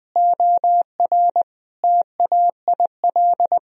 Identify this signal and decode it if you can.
Morse code, heard as a keyed tone, about 20 wpm, 700 Hz. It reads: OR TAIL
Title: Sentences from Top 1000 Words - Review Part 3 - 20wpm